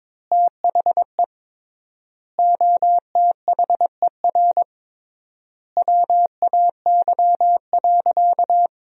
Morse code, heard as a keyed tone, about 23 wpm, 700 Hz. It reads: THE OTHER WAY.
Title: 3x Word-Spacing - Sentences from Top 100 Words - 22wpm